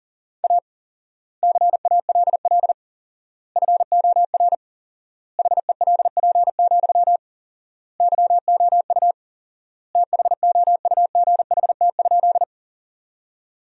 Morse code, heard as a keyed tone, about 40 wpm, 700 Hz. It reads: A CALL FOR HELP, YOU THOUGHT?
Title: 4x Word-Spacing - Sentences from Top 300 Words - 40wpm